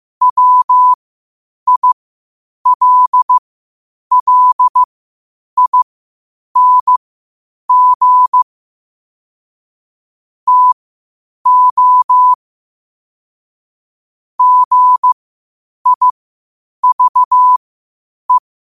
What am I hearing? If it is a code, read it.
Morse code, heard as a keyed tone, about 15 wpm, spaced as for 8 wpm, 1000 Hz. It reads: WILLING TO GIVE
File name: qotdfiles / 8wpm